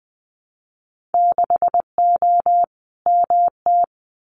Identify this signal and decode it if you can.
Morse code, heard as a keyed tone, about 20 wpm, 700 Hz. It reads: 6O MT
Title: Winter Field Day Exchanges Encore 20wpm